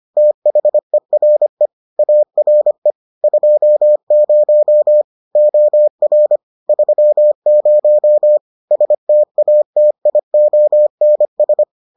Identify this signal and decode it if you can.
Morse code, heard as a keyed tone, about 25 wpm, 600 Hz. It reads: THERE ARE 20 OR 30 STATIONS